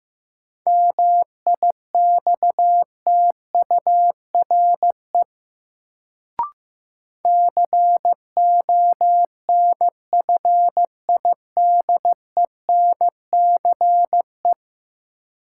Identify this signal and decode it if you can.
Morse code, heard as a keyed tone, about 15 wpm, 700 Hz. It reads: MIXTURE CONFIDENCE
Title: Sentences from Top 1500 Words - New Word Drill - 15wpm